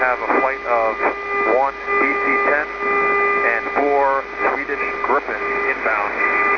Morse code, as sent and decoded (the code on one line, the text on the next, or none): none